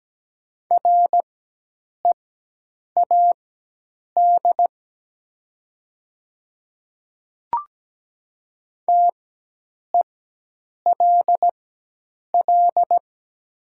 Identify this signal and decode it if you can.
Morse code, heard as a keyed tone, about 17 wpm, spaced as for 8 wpm, 700 Hz. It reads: READ TELL
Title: 4x Character Spacing - Top 500 Words - 17wpm